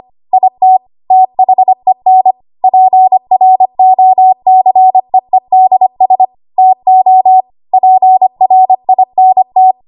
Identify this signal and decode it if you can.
Morse code, heard as a keyed tone, about 25 wpm, 750 Hz. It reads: IT THEN PROCEEDS TO PRINT
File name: Archive / 25 WPM / 25WPM